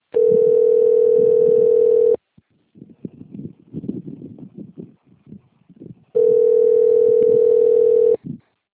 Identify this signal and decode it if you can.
Ringback tone (North America), 440 + 480 Hz.
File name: rg-1600-unknown- - - .46762